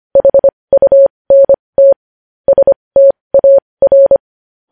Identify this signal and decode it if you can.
Morse code, heard as a keyed tone, about 25 wpm, spaced as for 20 wpm, 550 Hz. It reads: HUNT STAR
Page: News Headlines in Morse Code at 20 WPM